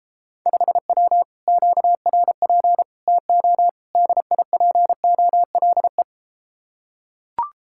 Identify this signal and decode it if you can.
Morse code, heard as a keyed tone, about 33 wpm, 700 Hz. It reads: 5W QRP TO DIPOLE